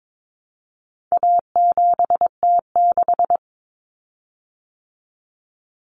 Morse code, heard as a keyed tone, about 22 wpm, 700 Hz. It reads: A7T6
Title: Common Cut Numbers - 4-Digit 22wpm